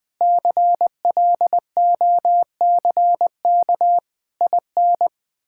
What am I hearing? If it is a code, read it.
Morse code, heard as a keyed tone, about 20 wpm, 700 Hz. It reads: CLOCK IN
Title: Pangram Challenge v2 - 20wpm